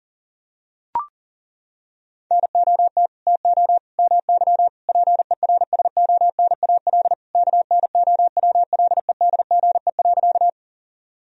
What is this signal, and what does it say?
Morse code, heard as a keyed tone, about 40 wpm, 700 Hz. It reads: NOT TO MY PERSONAL KNOWLEDGE.
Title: Sentences from Top 1000 Words - Review Part 4 - 40wpm